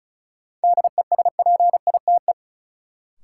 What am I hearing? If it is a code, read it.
Morse code, heard as a keyed tone, about 36 wpm, 700 Hz. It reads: DESPITE
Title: Sets of 4 Words - New Word Drill - 35wpm